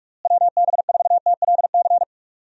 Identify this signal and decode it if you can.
Morse code, heard as a keyed tone, about 46 wpm, 700 Hz. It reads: WB4TLC
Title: USA Call Signs - 6 Characters - Repeated 2x - 45wpm